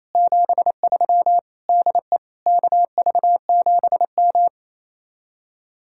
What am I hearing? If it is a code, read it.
Morse code, heard as a keyed tone, about 36 wpm, 700 Hz. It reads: 73 DE KV7M